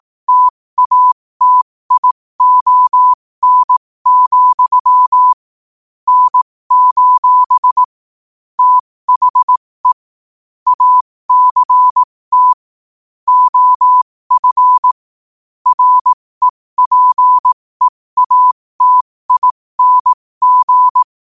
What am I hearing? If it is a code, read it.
Morse code, heard as a keyed tone, about 18 wpm, spaced as for 15 wpm, 1000 Hz. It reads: TATION, N: THE ACT OF REPEATING
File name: qotdfiles / 15wpm